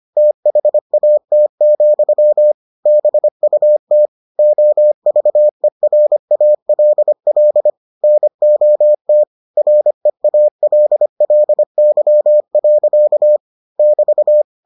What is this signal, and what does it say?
Morse code, heard as a keyed tone, about 25 wpm, 600 Hz. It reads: THAT, BUT OVERALL NOT REALLY. <BT>